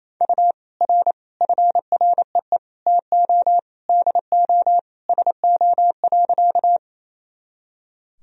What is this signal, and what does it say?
Morse code, heard as a keyed tone, about 28 wpm, 700 Hz. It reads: U R FREE TO DO SO.
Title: Sentences from Top 500 Words and Abbreviations 28wpm